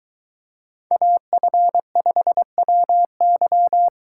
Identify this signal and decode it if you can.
Morse code, heard as a keyed tone, about 22 wpm, 700 Hz. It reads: AF5WY